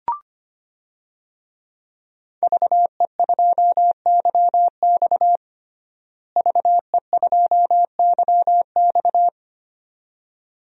Morse code, heard as a keyed tone, about 25 wpm, 700 Hz. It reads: VE2YX VE2YX
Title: DXCC-100 Call Signs - 5 Characters - Repeated 2x - 25wpm